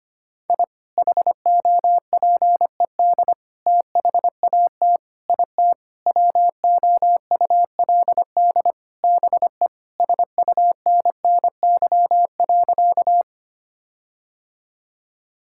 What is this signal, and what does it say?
Morse code, heard as a keyed tone, about 25 wpm, 700 Hz. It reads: I HOPED THAT IT WOULD BE SUNNY.
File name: practice / 25wpm / Sentences with 4-Word Phrases - 25wpm